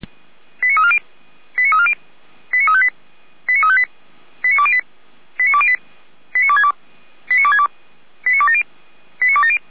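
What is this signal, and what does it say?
Five-tone selective call (ZVEI), heard as quick runs of five tones, tones of about 70 ms. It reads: 89340 89340 89358 89358 89208 89208 89262 89262 89270 89270
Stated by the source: ZVEI five-tone call received on an SDR.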